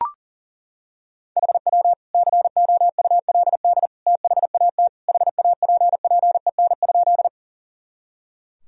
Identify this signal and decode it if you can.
Morse code, heard as a keyed tone, about 37 wpm, 700 Hz. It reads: HW COULD THAT HAPPEN?
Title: Sentences from Top 400 Words and Abbreviations 40wpm